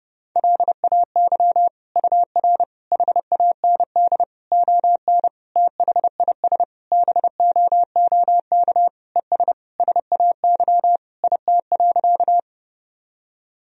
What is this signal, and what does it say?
Morse code, heard as a keyed tone, about 32 wpm, 700 Hz. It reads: LAY UR HAND ON THIS BOOK ES SAY IT.
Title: Sentences from Top 400 Words and Abbreviations 30wpm